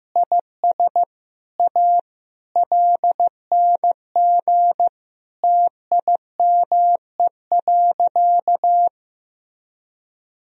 Morse code, heard as a keyed tone, about 15 wpm, 700 Hz. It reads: IS A LNG TIME.
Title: Sentences from Top 100 Words and Abbreviations 15wpm